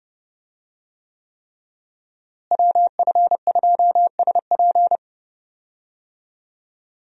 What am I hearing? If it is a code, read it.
Morse code, heard as a keyed tone, about 31 wpm, 700 Hz. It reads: WF2SP
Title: USA Call Signs - 5 Characters - Repeated 3x - 30wpm